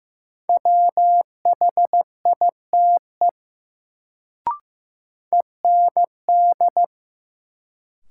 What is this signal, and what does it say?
Morse code, heard as a keyed tone, about 15 wpm, 700 Hz. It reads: WHITE END
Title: Sets of 2 Words - New Word Drill - 15wpm